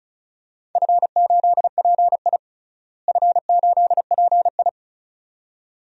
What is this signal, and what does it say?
Morse code, heard as a keyed tone, about 36 wpm, 700 Hz. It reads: F8PI F8PI
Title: DXCC-100 Call Signs - 4 Characters - Repeated 2x - 35wpm